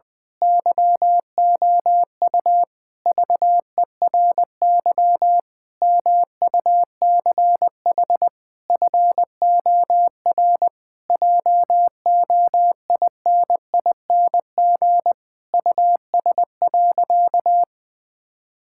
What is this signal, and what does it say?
Morse code, heard as a keyed tone, about 19 wpm, 700 Hz. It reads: YOU VERY MUCH FOR JOINING US.
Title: Sentences with 5-Word Phrases - 20wpm